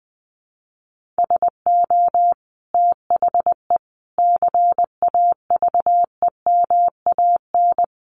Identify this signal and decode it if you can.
Morse code, heard as a keyed tone, about 20 wpm, 700 Hz. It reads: SO THE CAVEMAN
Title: A Sound Of Thunder 20wpm